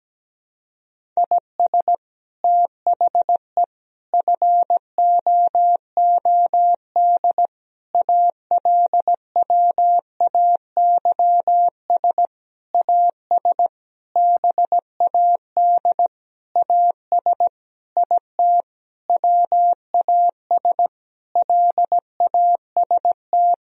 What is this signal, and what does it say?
Morse code, heard as a keyed tone, about 17 wpm, 700 Hz. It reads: IS THE FOOD ALWAYS AS BAD AS IT WAS LAST
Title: Sentences from Top 1000 Words - Review Part 1 - 17wpm